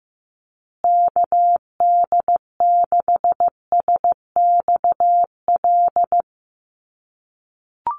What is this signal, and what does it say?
Morse code, heard as a keyed tone, about 14 wpm, 700 Hz. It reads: KD6SXL
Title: USA Call Signs Repeated 2x 15wpm